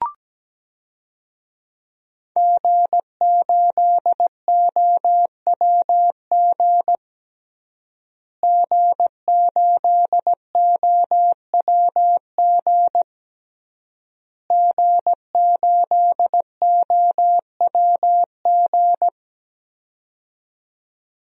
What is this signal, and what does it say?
Morse code, heard as a keyed tone, about 17 wpm, 700 Hz. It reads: G8OWG G8OWG G8OWG
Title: DXCC-100 Call Signs - 5 Characters - Repeated 3x - 17wpm